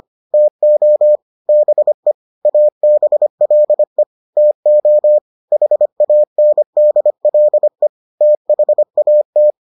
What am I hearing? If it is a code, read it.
Morse code, heard as a keyed tone, about 25 wpm, 600 Hz. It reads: TO BE ABLE TO HANDLE THAT